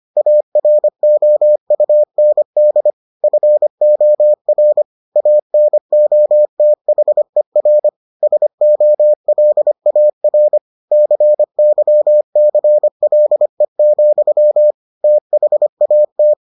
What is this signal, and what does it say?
Morse code, heard as a keyed tone, about 25 wpm, 600 Hz. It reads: AROUND FOR ANOTHER SOLAR CYCLE, THAT